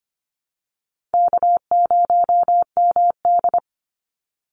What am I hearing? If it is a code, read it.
Morse code, heard as a keyed tone, about 25 wpm, 700 Hz. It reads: K0MD
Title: Easy Call Signs Repeated 3x - 25wpm